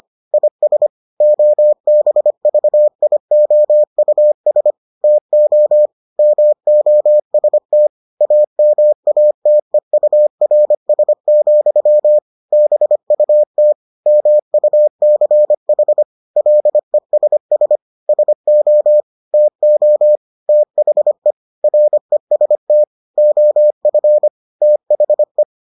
Morse code, heard as a keyed tone, about 25 wpm, 600 Hz. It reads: IS OBVIOUS TO MOST AMATEURS, BUT MUCH LESS SO TO THE REST OF THE